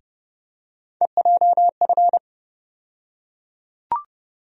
Morse code, heard as a keyed tone, about 30 wpm, 700 Hz. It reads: EJF
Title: USA Call Sign Postfixes - 3 Letters 30wpm